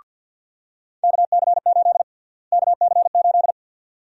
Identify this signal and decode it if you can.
Morse code, heard as a keyed tone, about 52 wpm, 700 Hz. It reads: KC8 KC8
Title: USA Call Sign Prefixes - Repeated 2x - 50wpm